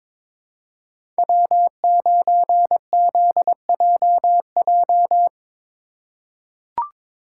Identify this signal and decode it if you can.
Morse code, heard as a keyed tone, about 22 wpm, 700 Hz. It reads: W9ZJJ
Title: USA Call Signs - 5 Characters - 3x Character Spacing - 22wpm